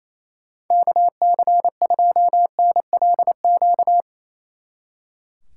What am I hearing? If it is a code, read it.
Morse code, heard as a keyed tone, about 34 wpm, 700 Hz. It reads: KC2NLQ
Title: Speed Racing - USA 2x3 Call Signs 28wpm